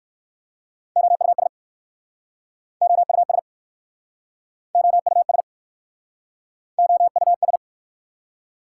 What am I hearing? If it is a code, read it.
Morse code, heard as a keyed tone, about 59 wpm, 700 Hz. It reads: OUS OUS OUS OUS